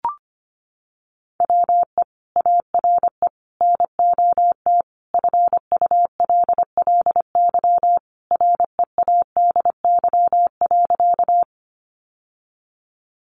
Morse code, heard as a keyed tone, about 25 wpm, 700 Hz. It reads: WE ARE NOT FULLY READY.
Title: Sentences from Top 1300 Words 25wpm